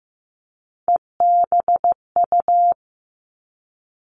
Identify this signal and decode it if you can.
Morse code, heard as a keyed tone, about 15 wpm, 700 Hz. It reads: EBU
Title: Three Challenging Letter-Number Combinations 15wpm